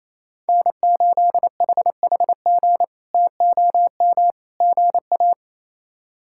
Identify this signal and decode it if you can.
Morse code, heard as a keyed tone, about 28 wpm, 700 Hz. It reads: N8HHG TOM GA
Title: K1USN SST QSO Practice - 28wpm